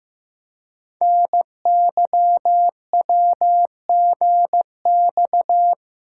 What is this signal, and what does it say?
Morse code, heard as a keyed tone, about 15 wpm, 700 Hz. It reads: NYWGX
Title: Code Groups - 5 Letter:Numbers - 15wpm